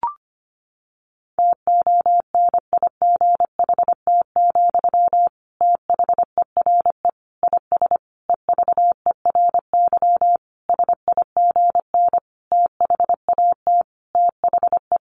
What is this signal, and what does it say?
Morse code, heard as a keyed tone, about 25 wpm, 700 Hz. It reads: TONIGHT, THERE IS EVERY SIGN THAT THE
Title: Sentences from Top 1400 Words 25wpm